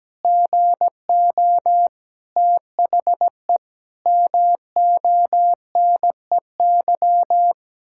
Morse code, heard as a keyed tone, about 17 wpm, 700 Hz. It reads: GO THE MONEY
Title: Pangram Challenge v2 - 17wpm